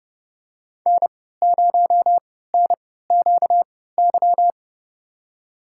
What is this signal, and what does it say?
Morse code, heard as a keyed tone, about 31 wpm, spaced as for 17 wpm, 700 Hz. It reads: N0NQY
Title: USA Call Signs - 5 Characters - 3x Character Spacing - 30wpm